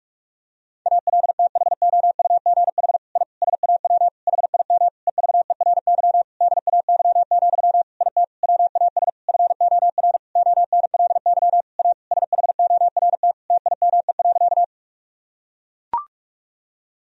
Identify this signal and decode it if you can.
Morse code, heard as a keyed tone, about 41 wpm, 700 Hz. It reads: ALTHOUGH I SAW HIM EVERY DAY, IT WAS FOR ONLY A SHORT TIME.
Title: Sentences from Top 1000 Words - Review Part 3 - 45wpm